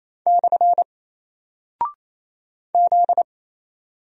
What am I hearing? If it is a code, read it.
Morse code, heard as a keyed tone, about 28 wpm, 700 Hz. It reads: / Z
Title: Single Character (Letter:Number:Punctuation:Prosign) 28wpm